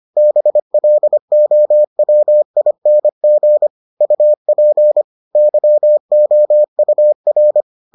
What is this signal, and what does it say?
Morse code, heard as a keyed tone, about 25 wpm, 600 Hz. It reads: BLOWING UP YOUR